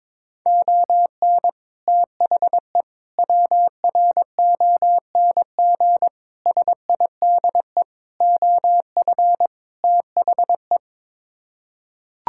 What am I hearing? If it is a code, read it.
Morse code, heard as a keyed tone, about 22 wpm, 700 Hz. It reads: ON THE WRONG SIDE OF THE
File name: practice / 22wpm / Sets of 6 Words - 22wpm